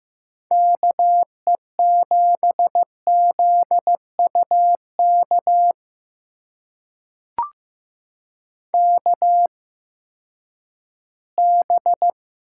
Morse code, heard as a keyed tone, about 15 wpm, spaced as for 10 wpm, 700 Hz. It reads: KE7ZUK KB
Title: USA Call Signs - 6 Characters - 8x Character Spacing - 15wpm